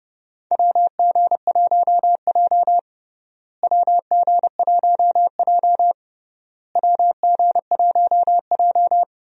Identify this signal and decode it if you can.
Morse code, heard as a keyed tone, about 30 wpm, 700 Hz. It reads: WG1J WG1J WG1J
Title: USA Call Signs - 4 Characters - Repeated 3x - 30wpm